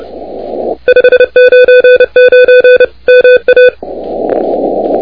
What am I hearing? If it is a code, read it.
Morse code, heard as a keyed tone, about 29 wpm, 1500 Hz, with noise. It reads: 599 MA
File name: recordings / W1WEF de NT1K ARRL10M CW 28MHz